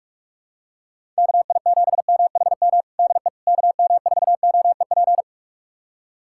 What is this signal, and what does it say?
Morse code, heard as a keyed tone, about 42 wpm, 700 Hz. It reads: KI7MHM DE KM4OEP
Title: USA Callsign DE Callsign - 6 Characters - 45wpm